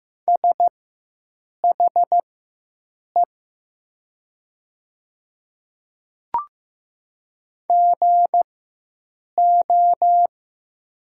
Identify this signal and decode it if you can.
Morse code, heard as a keyed tone, about 15 wpm, spaced as for 7 wpm, 700 Hz. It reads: SHE GO
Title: 4x Character Spacing - Top 200 Words - 15wpm